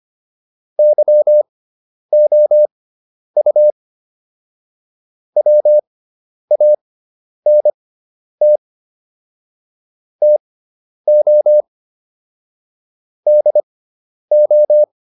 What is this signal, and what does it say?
Morse code, heard as a keyed tone, about 25 wpm, spaced as for 10 wpm, 600 Hz. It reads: YOU WANT TO DO
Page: intermediate-practice-files